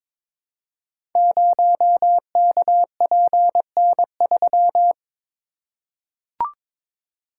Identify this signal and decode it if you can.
Morse code, heard as a keyed tone, about 23 wpm, 700 Hz. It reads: 0KPN3
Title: Code Groups - 5 Letter:Numbers - 22wpm